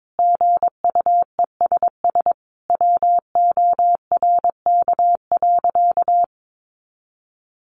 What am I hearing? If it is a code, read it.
Morse code, heard as a keyed tone, about 22 wpm, 700 Hz. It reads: GUESS WORK.
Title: Sentences from Top 1500 Words 22wpm